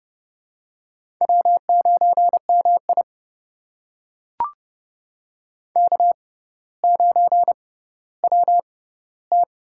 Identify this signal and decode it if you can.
Morse code, heard as a keyed tone, about 30 wpm, spaced as for 15 wpm, 700 Hz. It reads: W9MI K9WT
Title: USA Call Signs - 4 Characters - 6x Character Spacing - 30wpm